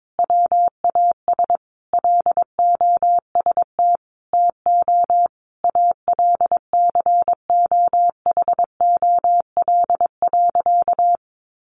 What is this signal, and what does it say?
Morse code, heard as a keyed tone, about 22 wpm, 700 Hz. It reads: WAS LOST TO ALCOHOL.
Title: Sentences from Top 1400 Words 22wpm